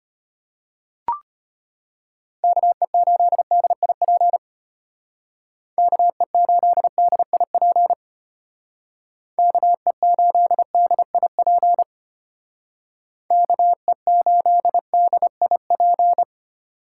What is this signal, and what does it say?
Morse code, heard as a keyed tone, about 30 wpm, 700 Hz. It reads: KE8DIP KE8DIP KE8DIP KE8DIP
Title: Speed Racing - USA 2x3 Call Signs 25wpm